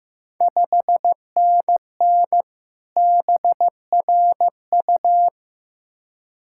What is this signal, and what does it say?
Morse code, heard as a keyed tone, about 15 wpm, 700 Hz. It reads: 5NN BRU
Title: Ontario QSO Party Exchange 15wpm